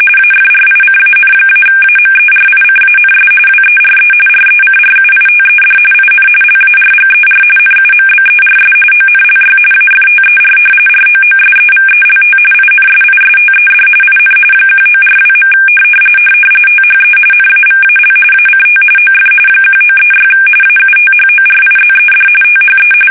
RTTY, about 45.45 baud. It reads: AFRICAN-AMERICANS WERE RECRUITED IN LARGER NUMBERS ONLY AFTER ROOSEVELT INSISTED IN 1942. THE NAVAL ACT OF 1936 AUTHORIZED THE
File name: fsk850